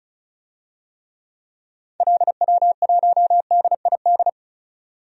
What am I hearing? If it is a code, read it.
Morse code, heard as a keyed tone, about 36 wpm, 700 Hz. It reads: LW1DID
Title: DXCC-100 Call Signs - 6 Characters - 35wpm